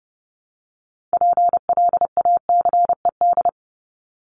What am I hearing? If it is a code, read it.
Morse code, heard as a keyed tone, about 30 wpm, 700 Hz. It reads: PLACED
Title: ED Suffix - Top 100 Words with Base Words 30wpm